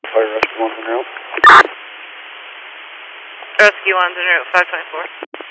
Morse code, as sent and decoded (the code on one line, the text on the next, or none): none